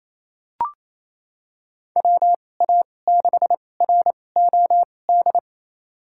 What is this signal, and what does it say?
Morse code, heard as a keyed tone, about 28 wpm, spaced as for 20 wpm, 700 Hz. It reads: WA6ROD
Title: USA Call Signs - 6 Characters - 2x Character Spacing - 28wpm